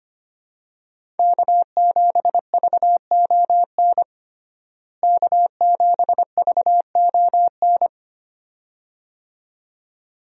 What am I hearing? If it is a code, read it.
Morse code, heard as a keyed tone, about 25 wpm, 700 Hz. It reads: K7VON K7VON
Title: USA Call Signs - 5 Characters - Repeated 2x - 25wpm